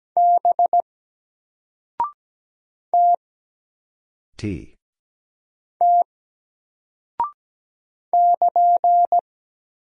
Morse code, heard as a keyed tone, about 17 wpm, 700 Hz. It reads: B T T <KN>
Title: Single Character (Letter:Number:Punctuation:Prosign) Alt 17wpm